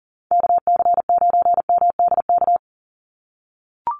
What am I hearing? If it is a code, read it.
Morse code, heard as a keyed tone, about 40 wpm, 700 Hz. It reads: KC9MDK
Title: USA Call Signs 40wpm